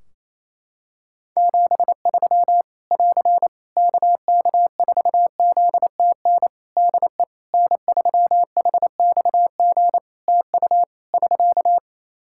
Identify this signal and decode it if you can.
Morse code, heard as a keyed tone, about 28 wpm, 700 Hz. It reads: 73 <AR> KK4ZTN DE N3HXG TU <SK>
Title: Practice QSO - Exchange 4 - Ending - 28wpm